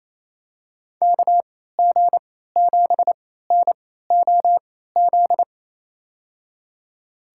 Morse code, heard as a keyed tone, about 28 wpm, spaced as for 16 wpm, 700 Hz. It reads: KG7NOZ
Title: USA Call Signs - 6 Characters - 3x Character Spacing - 28wpm